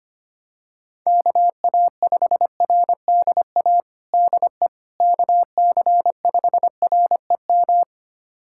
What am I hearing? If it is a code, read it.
Morse code, heard as a keyed tone, about 25 wpm, 700 Hz. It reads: KA5RDA DE KC5REM
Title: USA Callsign DE Callsign 25wpm